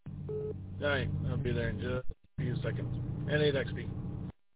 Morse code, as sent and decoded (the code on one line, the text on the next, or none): none